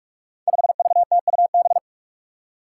Morse code, heard as a keyed tone, about 44 wpm, 700 Hz. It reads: 5VTUB
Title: Five Challenging Letter-Number Combinations 45wpm